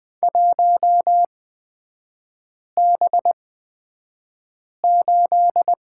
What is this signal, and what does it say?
Morse code, heard as a keyed tone, about 20 wpm, spaced as for 5 wpm, 690 Hz. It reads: 1B8
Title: 229 Lesson 37e - Letter x - Three Characters 20wpm